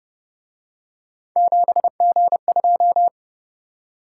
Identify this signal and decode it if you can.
Morse code, heard as a keyed tone, about 29 wpm, 700 Hz. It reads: 7G2